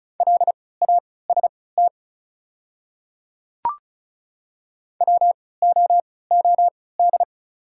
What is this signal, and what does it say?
Morse code, heard as a keyed tone, about 36 wpm, spaced as for 20 wpm, 700 Hz. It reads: LAST WOOD
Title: 3x Character Spacing - Top 300 Words - 35wpm